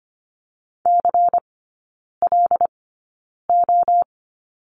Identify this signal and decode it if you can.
Morse code, heard as a keyed tone, about 25 wpm, spaced as for 9 wpm, 700 Hz. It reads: CLO
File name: course / 25:10wpm / 093 Lesson 16e - Letter c - Three Characters 25wpm